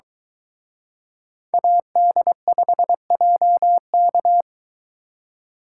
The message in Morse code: AD5JK

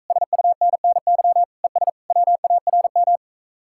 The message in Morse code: SUNNY ES WARM

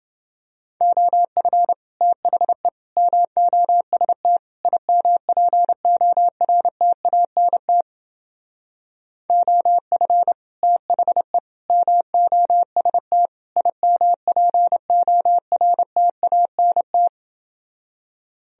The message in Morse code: OF THE MOST IMPORTANT OF THE MOST IMPORTANT

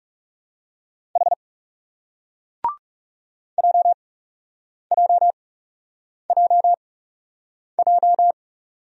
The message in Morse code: H JJJJ

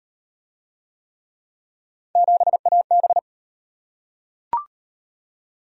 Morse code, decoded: 7AB

